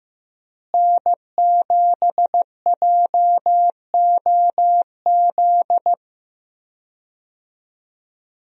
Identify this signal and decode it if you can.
Morse code, heard as a keyed tone, about 15 wpm, 700 Hz. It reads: N7JOZ
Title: USA Call Signs - 5 Characters - Repeated 2x - 15wpm